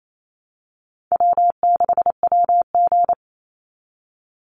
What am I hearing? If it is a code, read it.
Morse code, heard as a keyed tone, about 28 wpm, 700 Hz. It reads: W6WG
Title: Easy Call Signs Repeated 3x - 28wpm